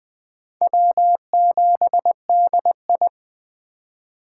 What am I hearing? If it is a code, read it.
Morse code, heard as a keyed tone, about 20 wpm, 700 Hz. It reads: W7DI
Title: USA Call Signs - 3x Character Spacing - 20wpm